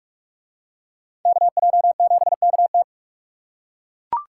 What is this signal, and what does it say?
Morse code, heard as a keyed tone, about 42 wpm, 700 Hz. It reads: KJ7KT